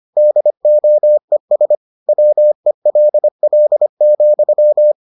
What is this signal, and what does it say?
Morse code, heard as a keyed tone, about 25 wpm, 600 Hz. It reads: DOES WELL,